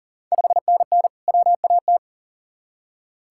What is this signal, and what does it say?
Morse code, heard as a keyed tone, about 40 wpm, 700 Hz. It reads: 5NN WAT
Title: Ontario QSO Party Exchange 40wpm